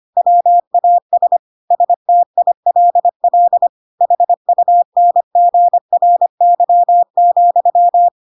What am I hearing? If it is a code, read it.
Morse code, heard as a keyed tone, about 25 wpm, 700 Hz. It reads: WAS STILL HUNGRY,
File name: Pinocchio / Pinocchio ch7 25 25 ik1hge